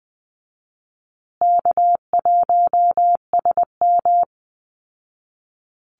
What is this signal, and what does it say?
Morse code, heard as a keyed tone, about 20 wpm, 700 Hz. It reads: K1SM
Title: Easy Call Signs Repeated 2x - 20wpm